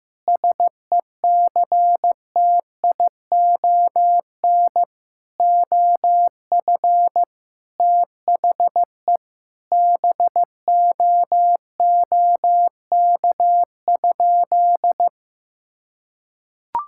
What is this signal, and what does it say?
Morse code, heard as a keyed tone, about 15 wpm, 700 Hz. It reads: SECTION OF THE BOOK?
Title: Sentences from Top 1000 Words - Review Part 5 - 15wpm